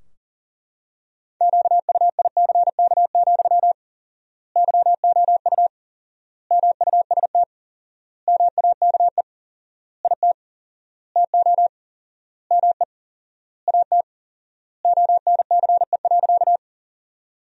Morse code, heard as a keyed tone, about 40 wpm, 700 Hz. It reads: QUICK, YOU MUST MAKE IT TO ME AT ONCE.
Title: 4x Word-Spacing - Sentences from Top 500 Words - 40wpm